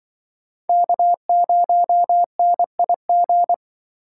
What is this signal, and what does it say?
Morse code, heard as a keyed tone, about 24 wpm, 700 Hz. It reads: K0NIG
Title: Speed Racing - USA 1x3 Call Signs 20wpm